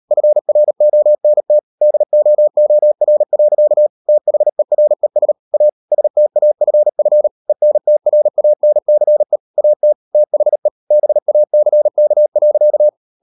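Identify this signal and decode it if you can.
Morse code, heard as a keyed tone, about 38 wpm, 600 Hz. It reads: FRONT DOOR. THERES A STAFF ENTRANCE AT THE BACK.
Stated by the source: Morse code practice at 38 WPM, sending Ken Follett's novel Never in English, from CW med Gnister.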